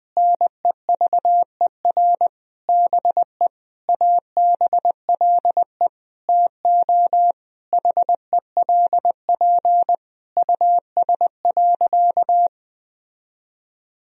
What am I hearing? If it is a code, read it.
Morse code, heard as a keyed tone, about 20 wpm, 700 Hz. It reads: NEVER BE ABLE TO HELP US.
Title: Sentences from Top 1000 Words - Review Part 2 - 20wpm